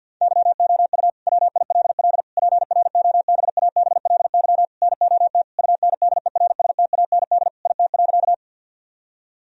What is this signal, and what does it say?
Morse code, heard as a keyed tone, about 50 wpm, 700 Hz. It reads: YOU WILL PROBABLY NOT UNDERSTAND IT.
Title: Sentences from Top 1000 Words - Review Part 4 - 50wpm